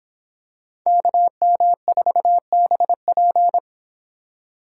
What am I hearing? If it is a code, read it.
Morse code, heard as a keyed tone, about 25 wpm, 700 Hz. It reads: KM4BP